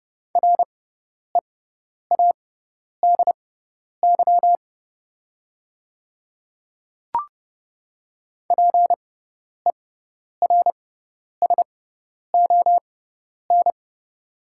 Morse code, heard as a keyed tone, about 30 wpm, spaced as for 10 wpm, 690 Hz. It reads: READY PERSON
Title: 6x Character Spacing - Top 500 Words - 30wpm